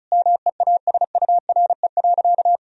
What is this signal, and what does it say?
Morse code, heard as a keyed tone, about 34 wpm, 700 Hz. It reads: MEASURE.
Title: Sentences from Top 1000 Words - Review Part 2 - 35wpm